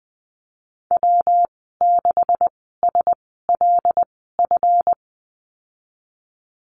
Morse code, heard as a keyed tone, about 19 wpm, spaced as for 14 wpm, 700 Hz. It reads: W6SLF
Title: 154 Lesson 25g - Letter p - Call Signs 20wpm